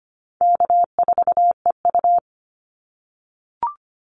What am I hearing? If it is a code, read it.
Morse code, heard as a keyed tone, about 25 wpm, 700 Hz. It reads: K4EU
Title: Call Signs - 25wpm